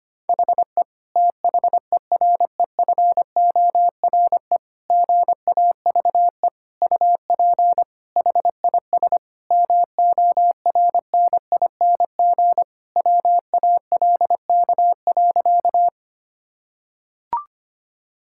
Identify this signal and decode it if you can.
Morse code, heard as a keyed tone, about 25 wpm, 700 Hz. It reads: HE THEREFORE GAVE UP HIS MORNING WALK.
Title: Sentences from Top 1000 Words - Review Part 5 - 25wpm